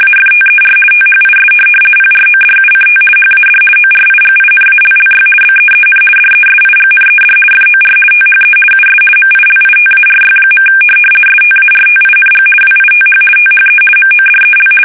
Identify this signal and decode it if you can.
RTTY, about 50 baud. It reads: AFTER THIS PRODUCT HAD ORIGINALLY BEEN DESIGNED FOR THE US MILITARY. STARTING WITH THE MODEL
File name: fsk850-50B / teletype-corp